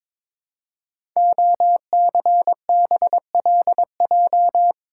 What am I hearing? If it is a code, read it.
Morse code, heard as a keyed tone, about 22 wpm, 700 Hz. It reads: OCBLJ
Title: Code Groups - 5 Letters - 22wpm